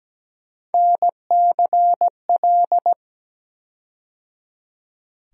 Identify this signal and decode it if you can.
Morse code, heard as a keyed tone, about 17 wpm, 700 Hz. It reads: NCL
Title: USA Call Sign Postfixes - 3 Letters 17wpm